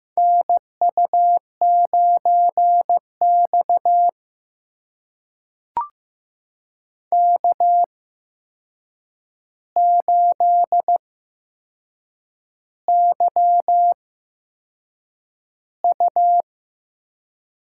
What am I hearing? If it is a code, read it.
Morse code, heard as a keyed tone, about 15 wpm, spaced as for 6 wpm, 700 Hz. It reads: NU9X K8YU
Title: USA Call Signs - 4 Characters - 8x Character Spacing - 15wpm